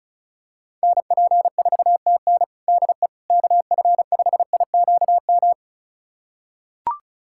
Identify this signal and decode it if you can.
Morse code, heard as a keyed tone, about 36 wpm, 700 Hz. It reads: NP4TN DE KF5IQM
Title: USA Callsign DE Callsign 35wpm